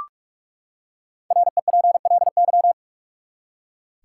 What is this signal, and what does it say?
Morse code, heard as a keyed tone, about 44 wpm, 700 Hz. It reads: REPLY
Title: Five Letter Words - 45wpm